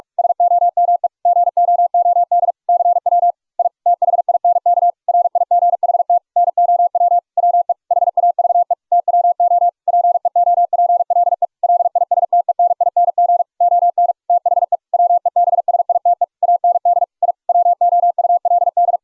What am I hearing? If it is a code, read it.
Morse code, heard as a keyed tone, about 41 wpm, 700 Hz. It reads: SOME GOOD CW I THINK RIGHT NOW WE HAVE TWO PEOPLE LISTENING ON THE WEBSITE AND I WOULD